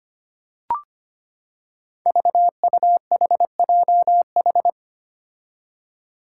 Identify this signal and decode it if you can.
Morse code, heard as a keyed tone, about 25 wpm, 700 Hz. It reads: VUHJH
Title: Five Challenging Letter-Number Combinations 25wpm